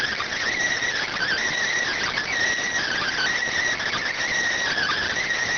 Emergency Alert System (SAME) tones (header bursts), heard as harsh buzzing bursts. It is heard from 1.0 to 5.3 s.